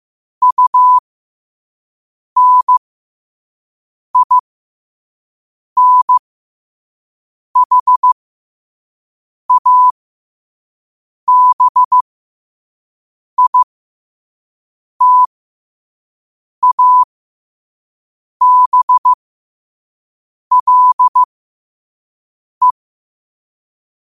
Morse code, heard as a keyed tone, about 15 wpm, spaced as for 5 wpm, 1000 Hz. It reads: UNINHABITABLE
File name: qotdfiles / 5wpm